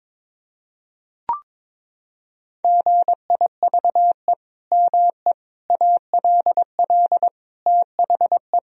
Morse code, heard as a keyed tone, about 22 wpm, 700 Hz. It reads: GIVE ME ALL THE
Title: Sentences from Top 1000 Words - Review Part 3 - 22wpm